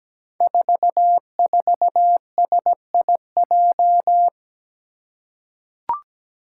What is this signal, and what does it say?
Morse code, heard as a keyed tone, about 17 wpm, 700 Hz. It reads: 44SIJ
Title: Five Challenging Letter-Number Combinations 17wpm